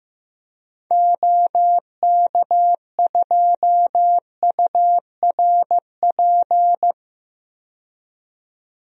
Morse code, heard as a keyed tone, about 16 wpm, 700 Hz. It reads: OK2URP
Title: DXCC-100 Call Signs - 6 Characters - 15wpm